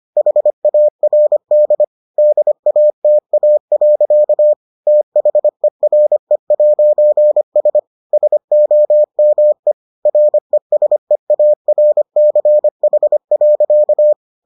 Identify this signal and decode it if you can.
Morse code, heard as a keyed tone, about 25 wpm, 600 Hz. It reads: HARD DATA. THERE'S SOME RESEARCH.